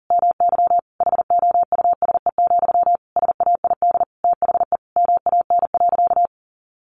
Morse code, heard as a keyed tone, about 40 wpm, 700 Hz. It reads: MY HOUSE, SAID THE MAN.
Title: Sentences from Top 400 Words 40wpm